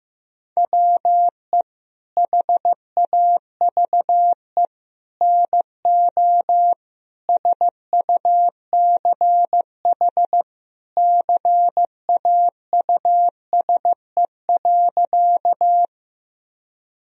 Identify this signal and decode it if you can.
Morse code, heard as a keyed tone, about 15 wpm, 700 Hz. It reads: WE HAVE NO SUCH CAUSE.
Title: Sentences from Top 1000 Words - Review Part 1 - 15wpm